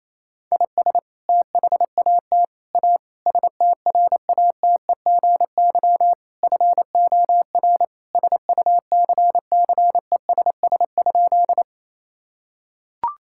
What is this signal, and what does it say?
Morse code, heard as a keyed tone, about 28 wpm, 700 Hz. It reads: IS THAT A STRATEGY FOR SUCCESS?
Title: Sentences from Top 1000 Words - Review Part 5 - 28wpm